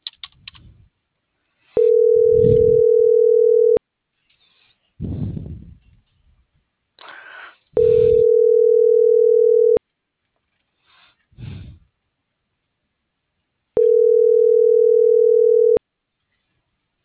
Ringback tone (North America), 440 + 480 Hz.